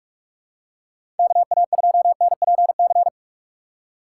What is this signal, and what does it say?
Morse code, heard as a keyed tone, about 46 wpm, 690 Hz. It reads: KA2NPC
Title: USA Call Signs - 6 Characters - 2x Character Spacing - 45wpm